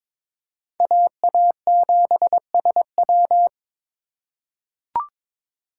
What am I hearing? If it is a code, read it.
Morse code, heard as a keyed tone, about 22 wpm, 700 Hz. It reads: AA7SW
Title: USA Call Signs - 5 Characters - 4x Character Spacing - 22wpm